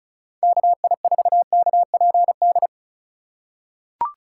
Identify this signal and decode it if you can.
Morse code, heard as a keyed tone, about 34 wpm, 700 Hz. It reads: KI4KPD
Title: USA Call Signs - 6 Characters - 3x Character Spacing - 35wpm